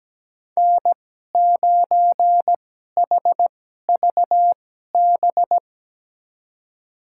Morse code, heard as a keyed tone, about 17 wpm, spaced as for 12 wpm, 700 Hz. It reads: N9HVB
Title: USA Call Signs - 5 Characters - 2x Character Spacing - 17wpm